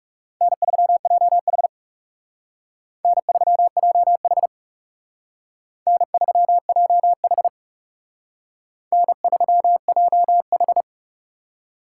Morse code, heard as a keyed tone, about 37 wpm, 700 Hz. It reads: N3JH N3JH N3JH N3JH
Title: Speed Racing - USA Call Signs 30wpm